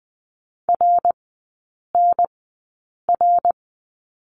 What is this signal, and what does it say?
Morse code, heard as a keyed tone, about 20 wpm, spaced as for 8 wpm, 690 Hz. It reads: RNR
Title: 166 Lesson 27e - Number 7 - Three Characters 20wpm